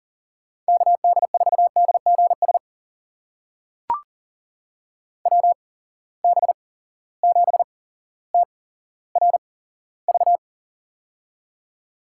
Morse code, heard as a keyed tone, about 40 wpm, spaced as for 17 wpm, 700 Hz. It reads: KD4DGS WB7TRV
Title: USA Call Signs - 6 Characters - 8x Character Spacing - 40wpm